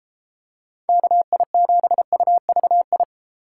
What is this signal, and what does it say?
Morse code, heard as a keyed tone, about 33 wpm, 700 Hz. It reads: KI7UVI